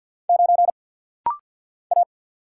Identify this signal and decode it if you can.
Morse code, heard as a keyed tone, about 48 wpm, 690 Hz. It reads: 9 A
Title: Single Letter-Number - Rapid-Fire 50wpm